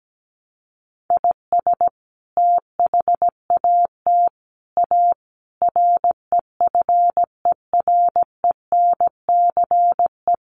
Morse code, heard as a keyed tone, about 17 wpm, 700 Hz. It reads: IS THAT A REFERENCE
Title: Sentences from Top 1200 Words 17wpm